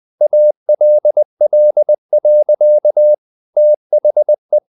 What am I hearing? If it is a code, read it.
Morse code, heard as a keyed tone, about 20 wpm, 600 Hz. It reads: ALL. THE
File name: the old man and the sea / the old man and the sea part 1 20wpm